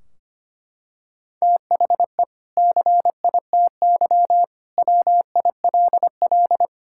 Morse code, heard as a keyed tone, about 25 wpm, 700 Hz. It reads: THE CITY WILL
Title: Sentences from Top 1600 Words 25wpm